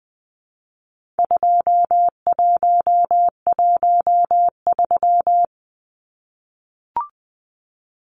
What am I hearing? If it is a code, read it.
Morse code, heard as a keyed tone, about 20 wpm, 700 Hz. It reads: 2113
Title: Numbers - 4-Digit 20wpm